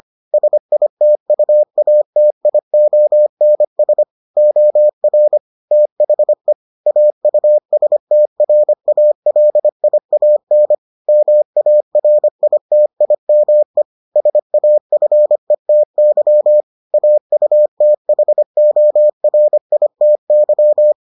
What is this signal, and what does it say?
Morse code, heard as a keyed tone, about 25 wpm, 600 Hz. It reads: SITUATIONS OR THE AUSTRALIAN MARITIME SAFETY AUTHORITY